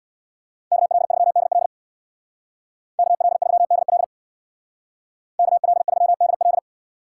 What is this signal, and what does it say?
Morse code, heard as a keyed tone, about 65 wpm, 700 Hz. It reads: KL3DL KL3DL KL3DL